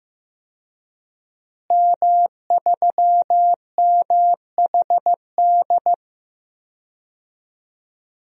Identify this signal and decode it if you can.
Morse code, heard as a keyed tone, about 15 wpm, 700 Hz. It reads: M3MHD
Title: DXCC-100 Call Signs - 5 Characters - Repeated 3x - 15wpm